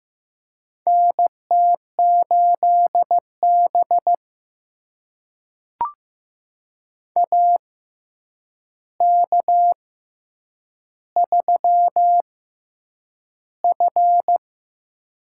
Morse code, heard as a keyed tone, about 15 wpm, spaced as for 8 wpm, 700 Hz. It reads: NT8B AK3F
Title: USA Call Signs - 6x Character Spacing - 15wpm